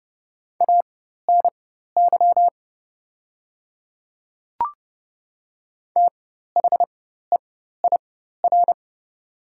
Morse code, heard as a keyed tone, about 30 wpm, spaced as for 14 wpm, 700 Hz. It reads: ANY THEIR
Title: 4x Character Spacing - Top 300 Words - 30wpm